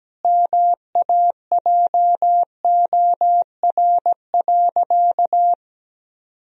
Morse code, heard as a keyed tone, about 17 wpm, 700 Hz. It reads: MAJOR.
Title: Sentences from Top 1000 Words - Review Part 4 - 17wpm